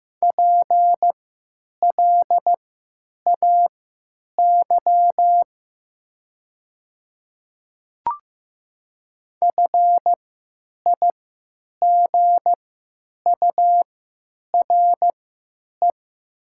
Morse code, heard as a keyed tone, about 15 wpm, spaced as for 9 wpm, 700 Hz. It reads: PLAY FIGURE
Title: 3x Character Spacing - Top 500 Words - 15wpm